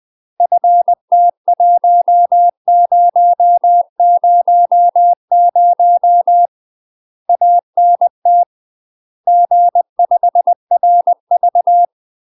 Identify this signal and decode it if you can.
Morse code, heard as a keyed tone, about 20 wpm, 700 Hz. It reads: FT1000 ANT G5RV
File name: UK MASTER QSOs TRAINING / 20WPM / UK MASTER QSOs 0042